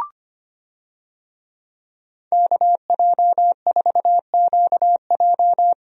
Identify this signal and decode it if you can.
Morse code, heard as a keyed tone, about 25 wpm, 700 Hz. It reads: KJ4QJ
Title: USA Call Signs - 5 Characters - 25wpm